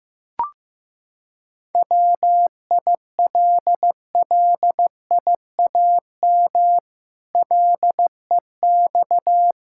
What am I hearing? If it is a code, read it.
Morse code, heard as a keyed tone, about 14 wpm, 700 Hz. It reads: WILLIAM LEX
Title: Pangram Challenge v2 - 15wpm